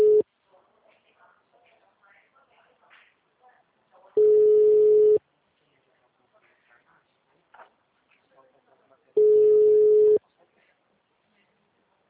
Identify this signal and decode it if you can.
Ringback tone (Europe, 425 Hz), with speech.